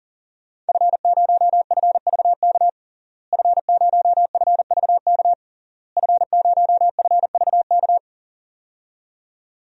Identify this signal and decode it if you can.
Morse code, heard as a keyed tone, about 39 wpm, 700 Hz. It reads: F0FVK F0FVK F0FVK
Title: DXCC-100 Call Signs - 5 Characters - Repeated 3x - 40wpm